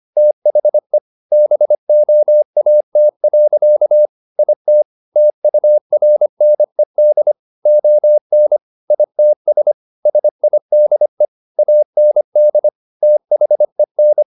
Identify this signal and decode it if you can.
Morse code, heard as a keyed tone, about 25 wpm, 600 Hz. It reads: THE BOAT. IT TURNED ON ITS SIDE AND THEN
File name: the old man and the sea part 1 25wpm